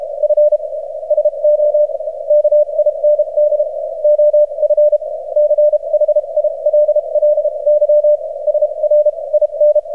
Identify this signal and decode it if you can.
Morse code, heard as a keyed tone, about 35 wpm, spaced as for 28 wpm, 600 Hz. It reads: F SO KIND OF CHILLY SR IN
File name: archived / CRSnet